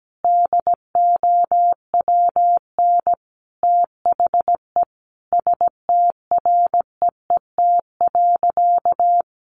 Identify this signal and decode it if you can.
Morse code, heard as a keyed tone, about 17 wpm, 700 Hz. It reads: DOWN THE STREET.
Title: Sentences from Top 1200 Words 17wpm